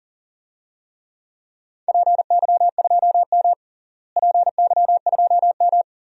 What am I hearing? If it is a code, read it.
Morse code, heard as a keyed tone, about 40 wpm, 700 Hz. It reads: PY2M PY2M
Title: DXCC-100 Call Signs - 4 Characters - Repeated 2x - 40wpm